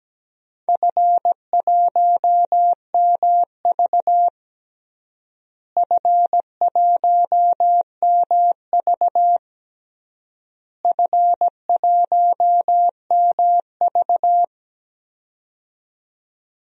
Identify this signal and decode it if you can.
Morse code, heard as a keyed tone, about 17 wpm, 700 Hz. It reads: F1MV F1MV F1MV
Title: DXCC-100 Call Signs - 4 Characters - Repeated 3x - 17wpm